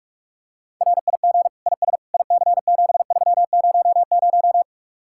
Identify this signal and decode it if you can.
Morse code, heard as a keyed tone, about 50 wpm, 700 Hz. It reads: RIG IS IC7300